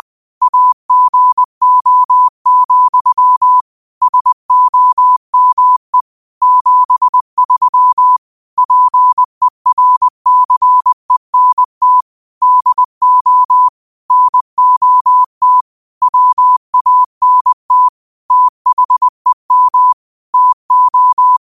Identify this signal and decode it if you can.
Morse code, heard as a keyed tone, about 20 wpm, 1000 Hz. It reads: AGO, SOME 73 PERCENT DO NOT WANT THEM TO
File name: qotdfiles / 20wpm